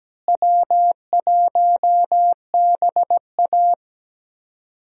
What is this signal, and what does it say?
Morse code, heard as a keyed tone, about 17 wpm, 700 Hz. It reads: W1BA
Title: USA Call Signs - 4 Characters - Repeated 2x - 17wpm